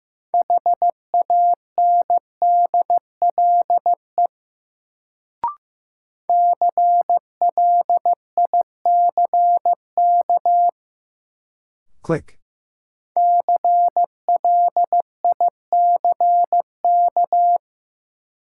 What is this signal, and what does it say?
Morse code, heard as a keyed tone, about 15 wpm, 700 Hz. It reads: HANDLE CLICK CLICK
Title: Sentences from Top 1100 Words - New Word Drill - 15wpm